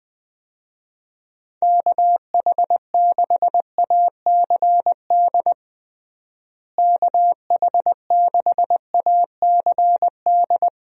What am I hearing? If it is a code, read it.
Morse code, heard as a keyed tone, about 21 wpm, 700 Hz. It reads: KH6ACD KH6ACD
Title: USA Call Signs - 6 Characters - Repeated 3x - 20wpm